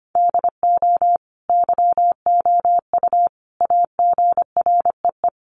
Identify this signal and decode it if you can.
Morse code, heard as a keyed tone, about 25 wpm, 700 Hz. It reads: DO YOU AGREE
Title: Sets of 5 Words Encore 25wpm